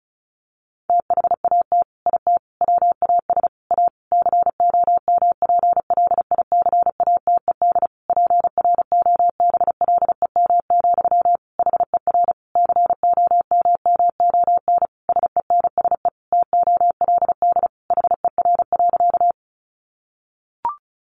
Morse code, heard as a keyed tone, about 34 wpm, 700 Hz. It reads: THAT IT WAS A COMPLICATED PROBLEM, HER COMMON SENSE TOLD HER.
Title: Sentences from Top 1200 Words 35wpm